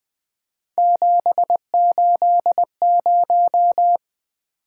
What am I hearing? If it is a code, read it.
Morse code, heard as a keyed tone, about 20 wpm, 700 Hz. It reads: 780
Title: Three Letter-Number Combinations - 20wpm